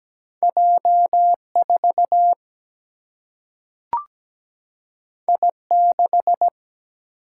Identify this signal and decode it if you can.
Morse code, heard as a keyed tone, about 17 wpm, 700 Hz. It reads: J4 I6
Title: Single-Letter Single-Number 17wpm